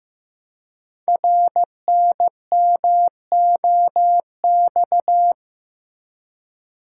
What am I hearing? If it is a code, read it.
Morse code, heard as a keyed tone, about 15 wpm, 700 Hz. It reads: RNMOX
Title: Code Groups - 5 Letters - 15wpm